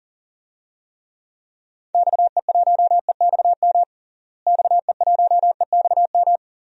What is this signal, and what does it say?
Morse code, heard as a keyed tone, about 40 wpm, 700 Hz. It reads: XE1EXM XE1EXM
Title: DXCC-100 Call Signs - 6 Characters - Repeated 3x - 40wpm